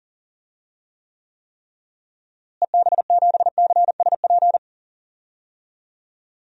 Morse code, heard as a keyed tone, about 40 wpm, 700 Hz. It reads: EB7CSP
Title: DXCC-100 Call Signs - 6 Characters - Repeated 3x - 40wpm